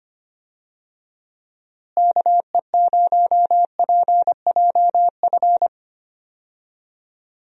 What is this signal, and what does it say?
Morse code, heard as a keyed tone, about 25 wpm, 700 Hz. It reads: KE0PJF
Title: USA Call Signs - 6 Characters - Repeated 3x - 25wpm